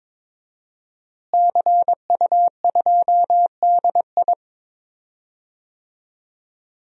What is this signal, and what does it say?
Morse code, heard as a keyed tone, about 22 wpm, 700 Hz. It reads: CU2DI
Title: DXCC-100 Call Signs - 5 Characters - Repeated 2x - 22wpm